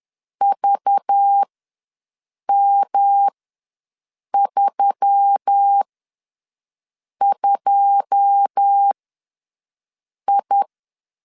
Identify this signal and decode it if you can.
Morse code, heard as a keyed tone, about 10 wpm, spaced as for 5 wpm, 800 Hz. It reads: VM32I
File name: LEZIONI / 9 28 ws 10wpm 4124